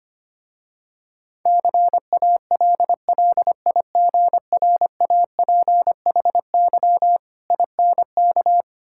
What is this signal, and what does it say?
Morse code, heard as a keyed tone, about 25 wpm, 700 Hz. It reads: CALLIGRAPHY INK
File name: practice / 25wpm / Difficult Words with Hints - 25wpm